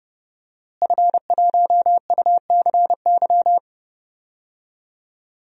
Morse code, heard as a keyed tone, about 30 wpm, 700 Hz. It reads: F1UCY
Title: DXCC-100 Call Signs - 5 Characters - Repeated 3x - 30wpm